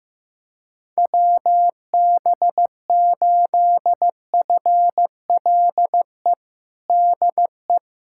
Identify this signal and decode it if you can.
Morse code, heard as a keyed tone, about 15 wpm, 700 Hz. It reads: WB8FLE DE
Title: USA Callsign DE Callsign - 6 Characters - 15wpm